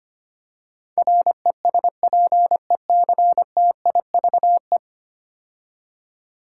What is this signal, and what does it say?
Morse code, heard as a keyed tone, about 25 wpm, 700 Hz. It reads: RESPECTIVE
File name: 10 Letter Words - First 3 Letters Given - 25wpm